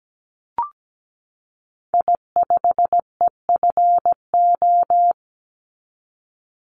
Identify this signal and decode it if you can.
Morse code, heard as a keyed tone, about 17 wpm, 700 Hz. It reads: I5EFO
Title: Call Signs - 17wpm